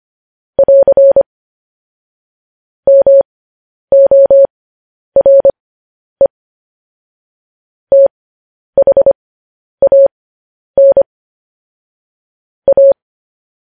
Morse code, heard as a keyed tone, about 25 wpm, spaced as for 10 wpm, 550 Hz. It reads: <AR> MORE THAN A